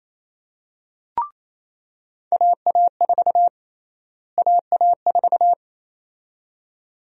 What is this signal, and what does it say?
Morse code, heard as a keyed tone, about 28 wpm, 700 Hz. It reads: AA4 AA4
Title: USA Call Sign Prefixes - Repeated 2x - 28wpm